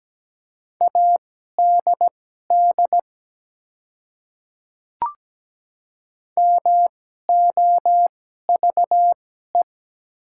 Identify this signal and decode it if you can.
Morse code, heard as a keyed tone, about 17 wpm, spaced as for 12 wpm, 700 Hz. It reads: ADD MOVE
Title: 2x Character Spacing - Top 200 Words - 17wpm